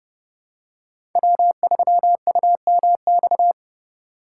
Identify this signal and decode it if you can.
Morse code, heard as a keyed tone, about 30 wpm, 700 Hz. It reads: W3UMX
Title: USA Call Signs - 5 Characters - 2x Character Spacing - 30wpm